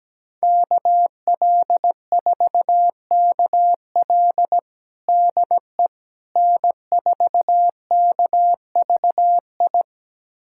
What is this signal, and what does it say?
Morse code, heard as a keyed tone, about 17 wpm, 700 Hz. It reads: KL4KL DE N4KVI
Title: USA Callsign DE Callsign - 5 Characters - 17wpm